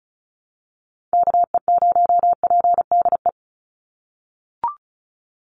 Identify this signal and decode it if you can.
Morse code, heard as a keyed tone, about 34 wpm, 700 Hz. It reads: KE0PDE
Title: USA Call Signs 35wpm